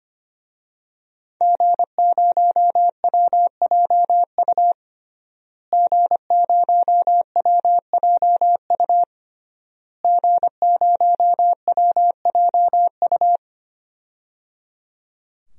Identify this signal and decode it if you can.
Morse code, heard as a keyed tone, about 25 wpm, 700 Hz. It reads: G0WJU G0WJU G0WJU
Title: DXCC-100 Call Signs - 5 Characters - Repeated 3x - 25wpm